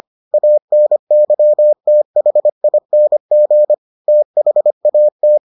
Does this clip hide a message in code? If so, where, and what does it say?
Yes: Morse code from 0.3 to 5.4 s, heard as a keyed tone, about 25 wpm, 600 Hz. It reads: ANYTHING THAT